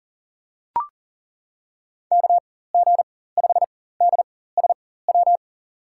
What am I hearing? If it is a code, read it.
Morse code, heard as a keyed tone, about 40 wpm, spaced as for 19 wpm, 700 Hz. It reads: KG5DSW
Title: USA Call Signs - 4x Character Spacing - 40wpm